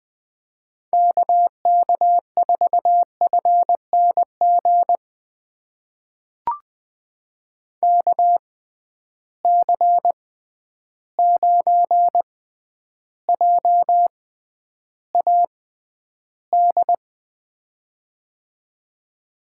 Morse code, heard as a keyed tone, about 20 wpm, spaced as for 10 wpm, 700 Hz. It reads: KK4FNG KC9JAD
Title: USA Call Signs - 6 Characters - 6x Character Spacing - 20wpm